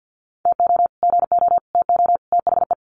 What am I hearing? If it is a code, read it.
Morse code, heard as a keyed tone, about 50 wpm, 700 Hz. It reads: TO GO TO THE
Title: Sets of 4 Words 50wpm